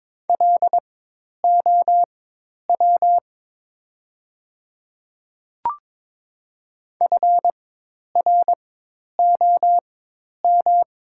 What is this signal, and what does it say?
Morse code, heard as a keyed tone, about 23 wpm, spaced as for 10 wpm, 700 Hz. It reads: LOW FROM
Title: 4x Character Spacing - Top 300 Words - 22wpm